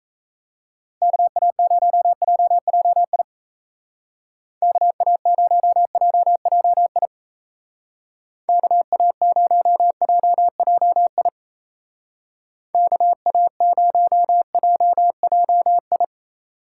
Morse code, heard as a keyed tone, about 35 wpm, 700 Hz. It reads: KA0JJI KA0JJI KA0JJI KA0JJI